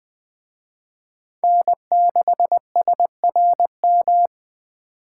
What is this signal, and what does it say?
Morse code, heard as a keyed tone, about 24 wpm, 700 Hz. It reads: N6SRM
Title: Speed Racing - USA 1x3 Call Signs 20wpm